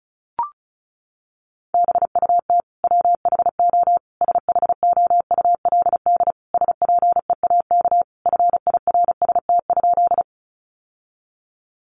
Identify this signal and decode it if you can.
Morse code, heard as a keyed tone, about 34 wpm, 700 Hz. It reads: BUT WHO SHOULD SPEAK FIRST?
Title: Sentences from Top 1100 Words 35wpm